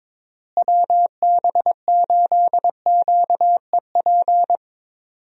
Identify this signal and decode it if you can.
Morse code, heard as a keyed tone, about 21 wpm, 700 Hz. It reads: WB8QEP
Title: USA Call Signs - 6 Characters - Repeated 2x - 22wpm